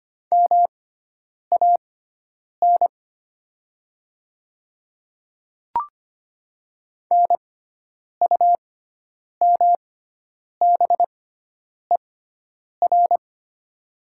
Morse code, heard as a keyed tone, about 25 wpm, spaced as for 9 wpm, 700 Hz. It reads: MAN NUMBER
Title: 6x Character Spacing - Top 300 Words - 25wpm